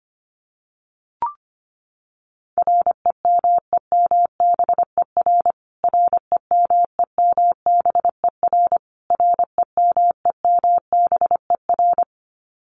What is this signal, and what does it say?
Morse code, heard as a keyed tone, about 25 wpm, 700 Hz. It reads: REMEMBER REMEMBER REMEMBER
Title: Top 500 Words Repeated 25wpm